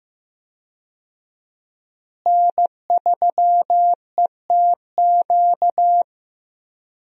Morse code, heard as a keyed tone, about 15 wpm, 700 Hz. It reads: N3ETQ
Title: USA Call Signs - 5 Characters - Repeated 2x - 15wpm